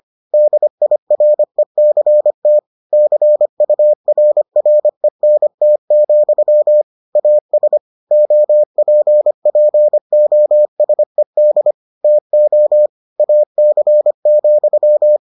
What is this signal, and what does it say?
Morse code, heard as a keyed tone, about 25 wpm, 600 Hz. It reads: DIRECT CURRENT, AS OPPOSED TO AC,